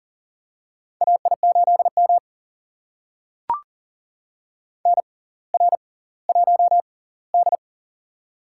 Morse code, heard as a keyed tone, about 38 wpm, spaced as for 20 wpm, 700 Hz. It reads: AI8M NR1D
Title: USA Call Signs - 4 Characters - 6x Character Spacing - 40wpm